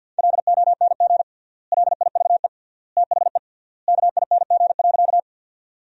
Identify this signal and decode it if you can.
Morse code, heard as a keyed tone, about 50 wpm, 700 Hz. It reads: LONG LIVE THE KING.
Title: 3x Word-Spacing - Sentences from Top 400 Words - 50wpm